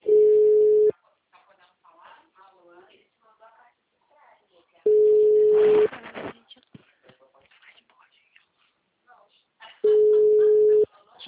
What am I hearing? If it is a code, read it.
Ringback tone (Europe, 425 Hz), with speech.